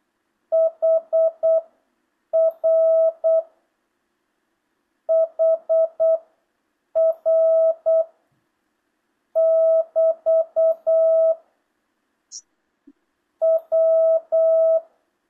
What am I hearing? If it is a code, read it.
Morse code, heard as a keyed tone, about 8 wpm, spaced as for 6 wpm, 650 Hz. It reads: HR HR <BT> W
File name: jh8cbh / 2025cw